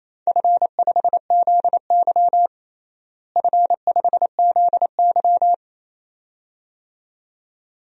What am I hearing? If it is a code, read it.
Morse code, heard as a keyed tone, about 28 wpm, 700 Hz. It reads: F5ZY F5ZY
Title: DXCC-100 Call Signs - 4 Characters - Repeated 2x - 28wpm